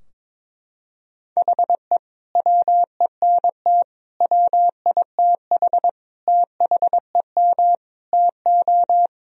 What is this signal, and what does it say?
Morse code, heard as a keyed tone, about 22 wpm, 690 Hz. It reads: HE WENT WITH THEM TO
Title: Sentences from Top 1000 Words - Review Part 3 - 22wpm